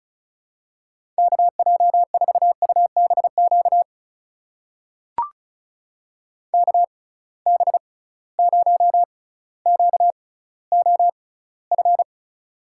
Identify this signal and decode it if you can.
Morse code, heard as a keyed tone, about 36 wpm, spaced as for 18 wpm, 700 Hz. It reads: KJ4UBQ KB0QOF
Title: USA Call Signs - 6 Characters - 6x Character Spacing - 35wpm